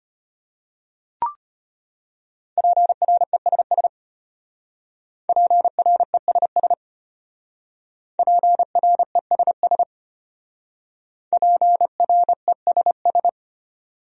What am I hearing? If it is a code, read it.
Morse code, heard as a keyed tone, about 32 wpm, 700 Hz. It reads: PRESS PRESS PRESS PRESS